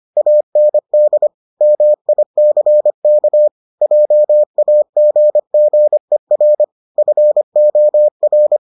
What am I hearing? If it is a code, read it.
Morse code, heard as a keyed tone, about 25 wpm, 600 Hz. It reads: AND MICK JAGGER FOR